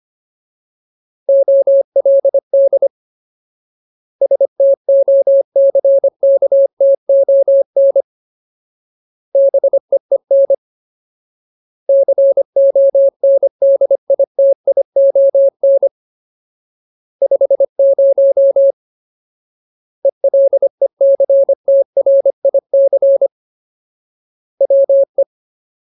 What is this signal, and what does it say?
Morse code, heard as a keyed tone, about 25 wpm, 550 Hz. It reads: OLD STOCKTON BEEN CONDITION 50 ELECTRIC WE
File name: Ultimate Word List / 25WPM Ultimate